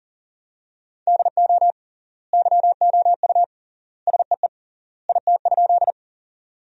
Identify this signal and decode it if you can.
Morse code, heard as a keyed tone, about 40 wpm, 700 Hz. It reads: DO YOU SEE IT?
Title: 3x Word-Spacing - Sentences from Top 100 Words - 40wpm